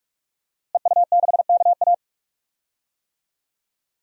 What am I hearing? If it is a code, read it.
Morse code, heard as a keyed tone, about 45 wpm, 700 Hz. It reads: EU6KA
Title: DXCC-100 Call Signs - 5 Characters - Repeated 2x - 45wpm